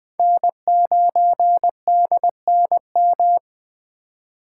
Morse code, heard as a keyed tone, about 21 wpm, 700 Hz. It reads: N9DNM